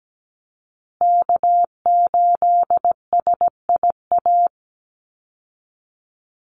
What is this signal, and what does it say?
Morse code, heard as a keyed tone, about 17 wpm, 700 Hz. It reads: K8SIA
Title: Call Signs Repeated 3x - 17wpm